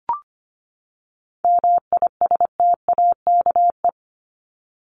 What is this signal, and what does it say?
Morse code, heard as a keyed tone, about 25 wpm, 700 Hz. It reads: MISTAKE
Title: Top 1501-1750 Words 25wpm